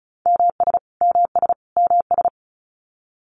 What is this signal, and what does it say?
Morse code, heard as a keyed tone, about 34 wpm, 700 Hz. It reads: MS MS MS
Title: US States Repeated 3x 35wpm